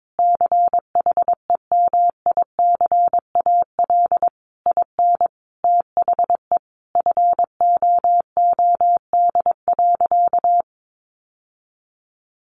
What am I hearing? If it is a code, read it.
Morse code, heard as a keyed tone, about 22 wpm, 700 Hz. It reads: CHEMICAL IN THE FOOD.
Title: Sentences from Top 1200 Words 22wpm